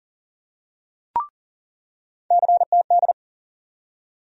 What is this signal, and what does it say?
Morse code, heard as a keyed tone, about 40 wpm, 700 Hz. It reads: CTD
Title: USA Call Sign Postfixes - 3 Letters 40wpm